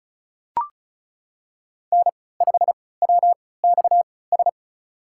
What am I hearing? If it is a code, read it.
Morse code, heard as a keyed tone, about 36 wpm, spaced as for 20 wpm, 700 Hz. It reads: N5WXS
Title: USA Call Signs - 5 Characters - 3x Character Spacing - 35wpm